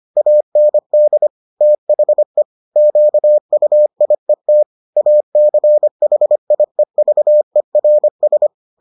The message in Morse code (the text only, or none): AND THE QUIET ACHIEVERS